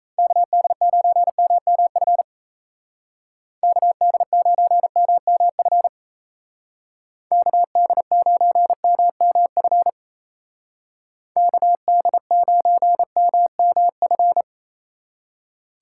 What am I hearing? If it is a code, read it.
Morse code, heard as a keyed tone, about 33 wpm, 700 Hz. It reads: KD9MMF KD9MMF KD9MMF KD9MMF
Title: Speed Racing - USA 2x3 Call Signs 28wpm